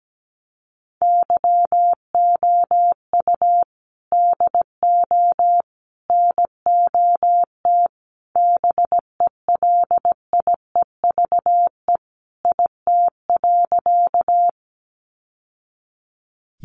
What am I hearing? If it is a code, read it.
Morse code, heard as a keyed tone, about 17 wpm, 700 Hz. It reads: YOU DO NOT BELIEVE IT.
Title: Sentences from Top 1000 Words 17wpm